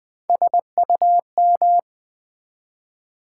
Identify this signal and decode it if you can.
Morse code, heard as a keyed tone, about 20 wpm, 700 Hz. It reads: SUM
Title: Build A Word - 3 Letters At A Time - 6-Letter Words - 20wpm